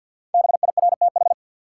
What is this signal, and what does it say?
Morse code, heard as a keyed tone, about 51 wpm, 700 Hz. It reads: BIRTH
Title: Sentences from Top 800 Words - New Word Drill - 50wpm